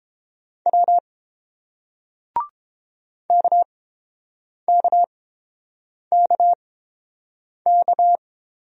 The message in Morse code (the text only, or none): W KKKK